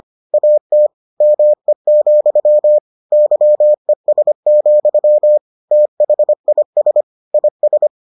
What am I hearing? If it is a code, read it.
Morse code, heard as a keyed tone, about 25 wpm, 600 Hz. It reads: AT ME, YES, THIS IS